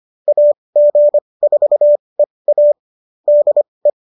Morse code, heard as a keyed tone, about 25 wpm, spaced as for 20 wpm, 590 Hz. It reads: AG4EA DE